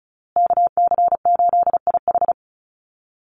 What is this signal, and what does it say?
Morse code, heard as a keyed tone, about 34 wpm, 700 Hz. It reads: KC8IH
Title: USA Call Signs Repeated 2x 35wpm